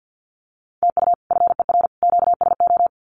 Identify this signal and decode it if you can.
Morse code, heard as a keyed tone, about 50 wpm, 700 Hz. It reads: TU FER QSO
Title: Common QSO Phrases 50wpm